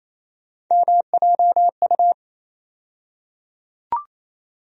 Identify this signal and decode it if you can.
Morse code, heard as a keyed tone, about 28 wpm, 700 Hz. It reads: MJU
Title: USA Call Sign Postfixes - 3 Letters 28wpm